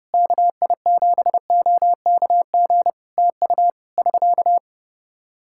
Morse code, heard as a keyed tone, about 30 wpm, 700 Hz. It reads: KI7OKG TU <SK>
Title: K1USN SST QSO Practice 30wpm